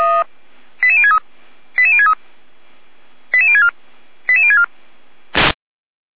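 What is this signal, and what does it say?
Five-tone selective call (ZVEI), heard as quick runs of five tones, tones of about 70 ms. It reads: 88063 88063 88064 88064
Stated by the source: ZVEI five-tone selective call 88367 received on an SDR.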